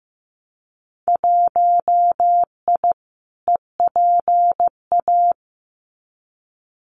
Morse code, heard as a keyed tone, about 15 wpm, 700 Hz. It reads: 1I EPA
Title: Winter Field Day Exchanges Encore 15wpm